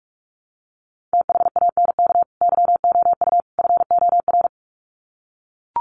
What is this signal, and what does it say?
Morse code, heard as a keyed tone, about 44 wpm, 700 Hz. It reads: THANK YOU FOR
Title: Sets of 3 Words 45wpm